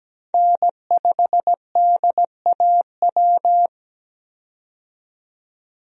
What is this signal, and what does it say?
Morse code, heard as a keyed tone, about 17 wpm, 700 Hz. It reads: N5DAW
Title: USA Call Signs - 5 Characters - 17wpm